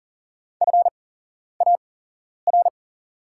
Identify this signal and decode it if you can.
Morse code, heard as a keyed tone, about 37 wpm, spaced as for 11 wpm, 700 Hz. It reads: FAR
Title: 8x Character Spacing - Top 500 Words - 40wpm